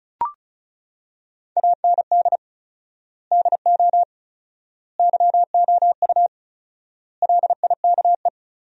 Morse code, heard as a keyed tone, about 34 wpm, 700 Hz. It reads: AND DO YOU LIKE
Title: 4x Word-Spacing - Sentences from Top 100 Words - 35wpm